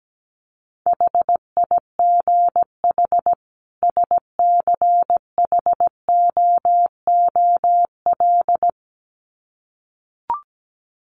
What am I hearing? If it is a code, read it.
Morse code, heard as a keyed tone, about 17 wpm, 700 Hz. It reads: HIGH SCHOOL
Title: Sets of 2 Words 17wpm